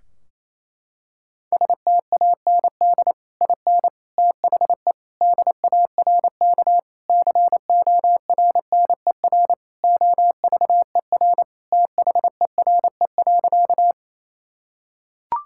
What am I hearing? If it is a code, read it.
Morse code, heard as a keyed tone, about 28 wpm, 700 Hz. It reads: STAND IN THE DARK CORNER OVER THERE.
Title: Sentences from Top 1600 Words 28wpm